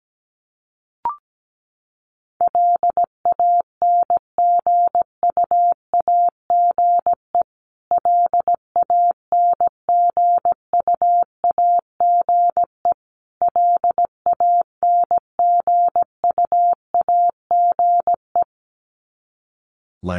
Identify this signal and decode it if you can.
Morse code, heard as a keyed tone, about 17 wpm, 700 Hz. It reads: LANGUAGE LANGUAGE LANGUAGE
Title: Top 500 Words Repeated 17wpm